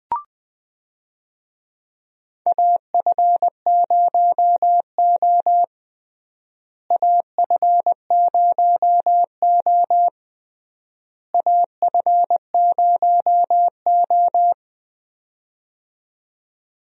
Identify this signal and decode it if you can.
Morse code, heard as a keyed tone, about 20 wpm, 700 Hz. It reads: AF0O AF0O AF0O
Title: USA Call Signs - 4 Characters - Repeated 3x - 20wpm